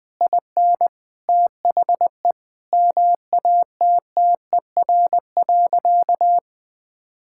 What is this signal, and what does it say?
Morse code, heard as a keyed tone, about 21 wpm, 700 Hz. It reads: IN THE MATTER.
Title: Sentences from Top 1000 Words - Review Part 5 - 20wpm